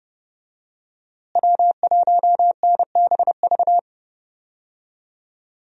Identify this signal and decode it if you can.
Morse code, heard as a keyed tone, about 31 wpm, 700 Hz. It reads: W1NBV
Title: USA Call Signs - 5 Characters - Repeated 2x - 30wpm